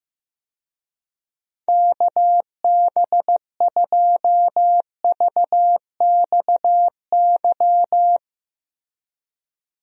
Morse code, heard as a keyed tone, about 15 wpm, 700 Hz. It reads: KB2VXY
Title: USA Call Signs - 6 Characters - Repeated 3x - 15wpm